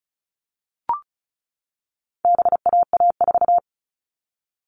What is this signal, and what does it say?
Morse code, heard as a keyed tone, about 34 wpm, 700 Hz. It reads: BAA4
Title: All Cut Numbers - 4-Digit 35wpm